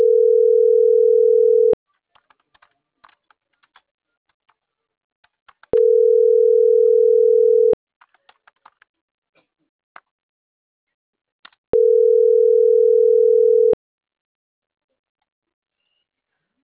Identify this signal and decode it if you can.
Ringback tone (North America), 440 + 480 Hz.